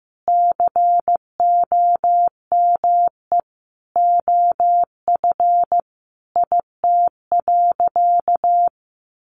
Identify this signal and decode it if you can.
Morse code, heard as a keyed tone, about 15 wpm, 700 Hz. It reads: COME OF IT.
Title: Sentences from Top 1100 Words 15wpm